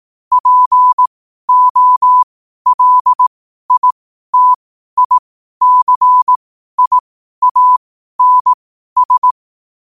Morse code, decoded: POLITICIANS

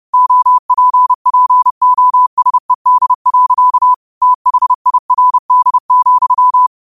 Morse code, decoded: OPPOSED. THIRD,

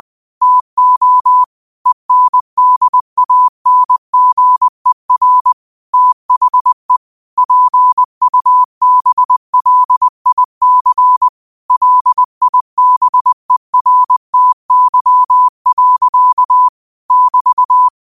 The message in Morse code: TO ENDANGER THE PUBLIC LIBERTY. <BT>

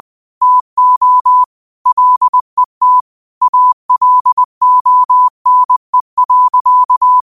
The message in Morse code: TO LET ALONE.

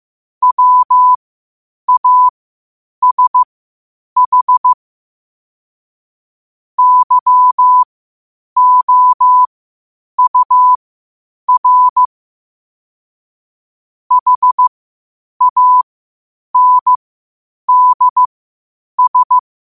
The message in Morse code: WASH YOUR HANDS

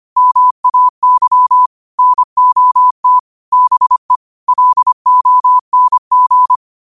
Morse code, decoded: MAY NOT BE LONG

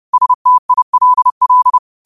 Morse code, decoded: STILL

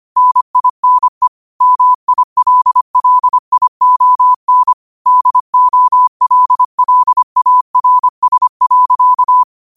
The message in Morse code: NINE MILLION DOLLARS.

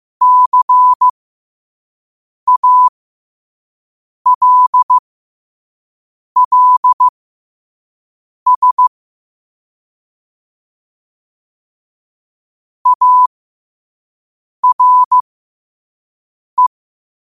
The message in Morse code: CALLS ARE

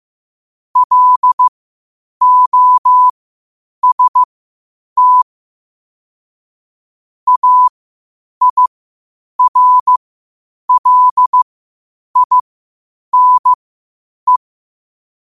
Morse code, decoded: LOST AIRLINE

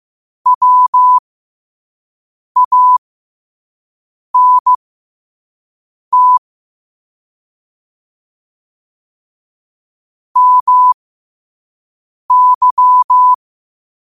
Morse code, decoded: WANT MY

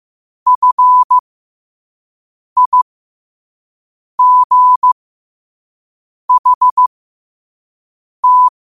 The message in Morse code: FIGHT

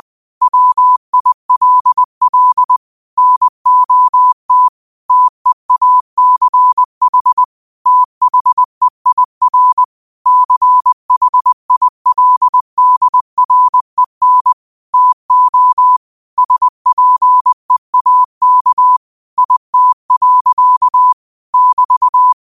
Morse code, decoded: WILL NOT TEACH THEIR CHILDREN TO SPEAK IT. <BT>

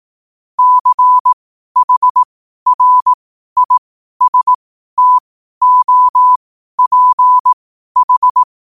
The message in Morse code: CHRISTOPH

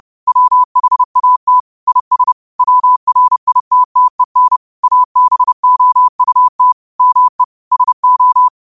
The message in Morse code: WHAT IS WRITTEN ABOUT ME SO

